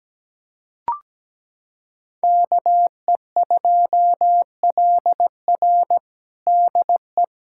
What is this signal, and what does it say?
Morse code, heard as a keyed tone, about 17 wpm, 700 Hz. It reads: KE2LR DE
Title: USA Callsign DE Callsign - 5 Characters - 17wpm